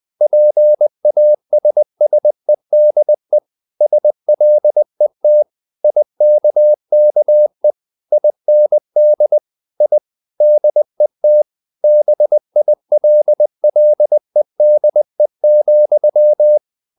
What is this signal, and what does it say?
Morse code, heard as a keyed tone, about 20 wpm, 600 Hz. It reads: PASSEDE SLET IKKE IND I DET BILLEDE,
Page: Ferie 20 wpm